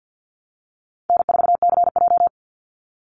Morse code, heard as a keyed tone, about 50 wpm, 700 Hz. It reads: N4CJ
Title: Call Signs - 50wpm